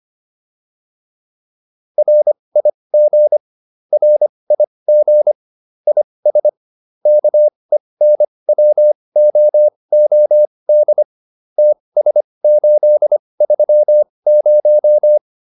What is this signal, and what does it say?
Morse code, heard as a keyed tone, about 25 wpm, spaced as for 20 wpm, 600 Hz. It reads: RIG RIG IS KENWOOD TS830
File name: QSO406 20